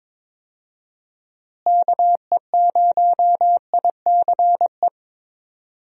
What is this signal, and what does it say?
Morse code, heard as a keyed tone, about 21 wpm, 700 Hz. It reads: KE0ICE
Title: USA Call Signs - 6 Characters - Repeated 3x - 22wpm